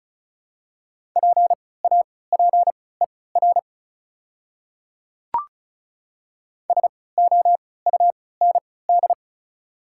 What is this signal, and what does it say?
Morse code, heard as a keyed tone, about 34 wpm, spaced as for 20 wpm, 690 Hz. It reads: PAPER SOUND
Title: 3x Character Spacing - Top 500 Words - 35wpm